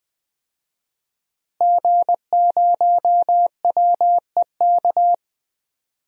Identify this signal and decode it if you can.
Morse code, heard as a keyed tone, about 20 wpm, 700 Hz. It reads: G0WEK
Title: DXCC-100 Call Signs - 5 Characters - Repeated 2x - 20wpm